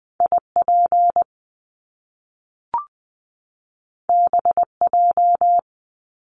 Morse code, heard as a keyed tone, about 20 wpm, 700 Hz. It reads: IP BJ
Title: Top Two Letter Combinations in Words 20wpm